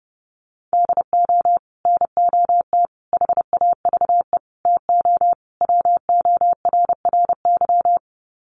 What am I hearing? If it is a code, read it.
Morse code, heard as a keyed tone, about 30 wpm, 700 Hz. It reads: DO NOT HAVE TO WORRY
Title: Sets of 5 Words 30wpm